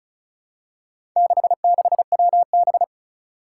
Morse code, heard as a keyed tone, about 36 wpm, 700 Hz. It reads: 66WB
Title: Four Challenging Letter-Number Combinations 35wpm